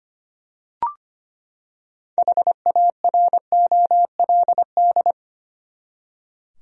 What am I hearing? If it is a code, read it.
Morse code, heard as a keyed tone, about 25 wpm, 700 Hz. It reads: HAROLD
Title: USA Common Names - 25wpm